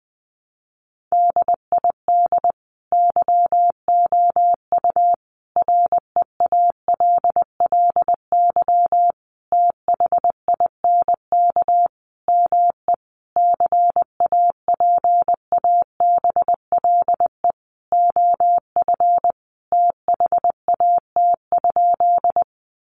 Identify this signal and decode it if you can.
Morse code, heard as a keyed tone, about 20 wpm, 700 Hz. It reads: DID YOU REALLY THINK ME CAPABLE OF THAT?
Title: Sentences from Top 1200 Words 20wpm